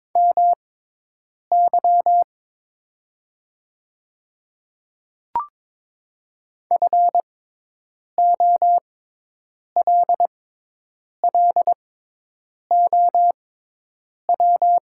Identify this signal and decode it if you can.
Morse code, heard as a keyed tone, about 22 wpm, spaced as for 8 wpm, 700 Hz. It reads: MY FOLLOW
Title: 6x Character Spacing - Top 200 Words - 22wpm